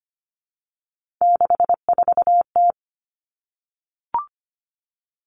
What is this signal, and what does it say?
Morse code, heard as a keyed tone, about 25 wpm, 700 Hz. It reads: 64T
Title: Common Cut Numbers - 3-Digit 25wpm